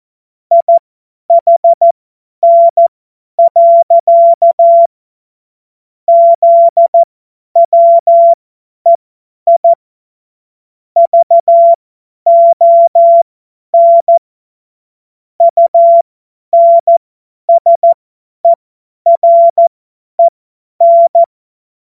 Morse code, heard as a keyed tone, about 14 wpm, spaced as for 10 wpm, 680 Hz. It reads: IHN. ZWEI VON UNSEREN
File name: Morsekurs / Witze / Witz 01. November 2022 50 F70